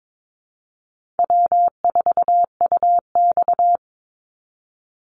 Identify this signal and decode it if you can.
Morse code, heard as a keyed tone, about 22 wpm, 700 Hz. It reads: W4UX
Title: Easy Call Signs Repeated 3x - 22wpm